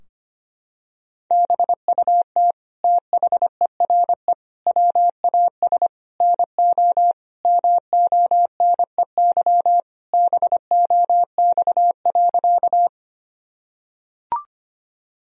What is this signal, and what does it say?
Morse code, heard as a keyed tone, about 25 wpm, 700 Hz. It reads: BUT THERE WAS NO MONEY BOX.
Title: Sentences from Top 1000 Words - Review Part 2 - 25wpm